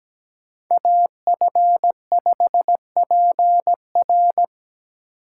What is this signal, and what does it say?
Morse code, heard as a keyed tone, about 17 wpm, 700 Hz. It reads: AF5PR
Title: USA Call Signs - 5 Characters - 2x Character Spacing - 17wpm